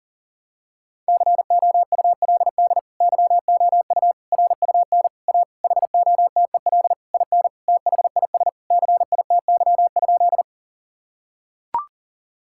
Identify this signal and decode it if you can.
Morse code, heard as a keyed tone, about 37 wpm, 700 Hz. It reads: COULD YOU RUN A HOTEL IN THIS CITY?
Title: Sentences from Top 1600 Words 40wpm